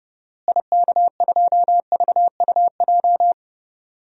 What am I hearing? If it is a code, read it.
Morse code, heard as a keyed tone, about 32 wpm, 700 Hz. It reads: IK2VUJ
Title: DXCC-100 Call Signs - 6 Characters - Repeated 3x - 30wpm